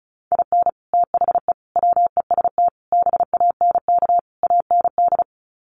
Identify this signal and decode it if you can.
Morse code, heard as a keyed tone, about 34 wpm, 700 Hz. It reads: IN THE WEST BANK AND
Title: Sets of 5 Words Encore2 35wpm